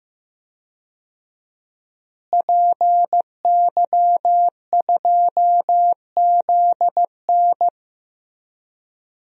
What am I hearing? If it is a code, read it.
Morse code, heard as a keyed tone, about 15 wpm, 700 Hz. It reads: PY2ZN
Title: DXCC-100 Call Signs - 5 Characters - Repeated 3x - 15wpm